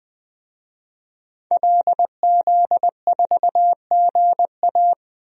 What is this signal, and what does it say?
Morse code, heard as a keyed tone, about 20 wpm, 700 Hz. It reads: LZ4GA
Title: DXCC-100 Call Signs - 5 Characters - 20wpm